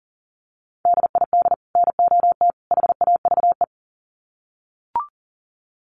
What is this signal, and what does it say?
Morse code, heard as a keyed tone, about 40 wpm, 700 Hz. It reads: DID NOT HAVE
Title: Sets of 3 Words 40wpm